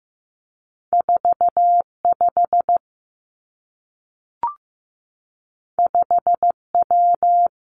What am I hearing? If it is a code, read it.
Morse code, heard as a keyed tone, about 15 wpm, 700 Hz. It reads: 4 5 5 W
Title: Two Challenging Letter-Number Combinations 15wpm